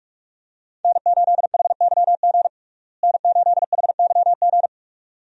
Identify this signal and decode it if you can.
Morse code, heard as a keyed tone, about 46 wpm, 700 Hz. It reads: N8HYG N8HYG
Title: USA Call Signs - 5 Characters - Repeated 2x - 45wpm